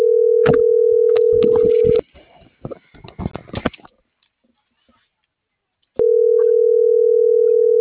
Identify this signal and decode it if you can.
Ringback tone (North America), 440 + 480 Hz.